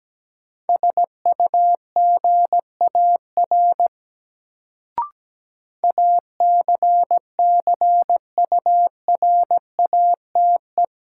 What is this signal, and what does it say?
Morse code, heard as a keyed tone, about 17 wpm, 700 Hz. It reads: SUGAR ACCURATE
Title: Sentences from Top 1200 Words - New Word Drill - 17wpm